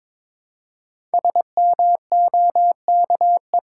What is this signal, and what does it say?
Morse code, heard as a keyed tone, about 22 wpm, 700 Hz. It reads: SMOKE
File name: Difficult Sentences with Hints - 22wpm